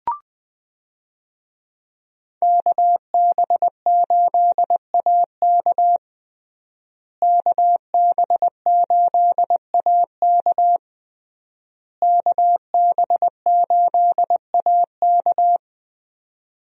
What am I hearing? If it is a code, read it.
Morse code, heard as a keyed tone, about 21 wpm, 700 Hz. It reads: KB8AK KB8AK KB8AK
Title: USA Call Signs - 5 Characters - Repeated 3x - 20wpm